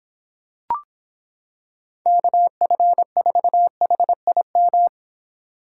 Morse code, heard as a keyed tone, about 26 wpm, 700 Hz. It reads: KF4HIM